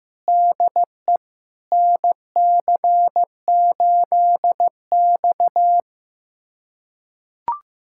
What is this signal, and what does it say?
Morse code, heard as a keyed tone, about 15 wpm, 700 Hz. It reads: DE NC8X
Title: USA Callsign DE Callsign - 4 Characters - 15wpm